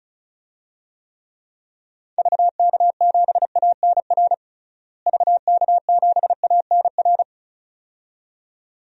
Morse code, heard as a keyed tone, about 36 wpm, 700 Hz. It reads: VK7ANR VK7ANR
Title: DXCC-100 Call Signs - 6 Characters - Repeated 2x - 35wpm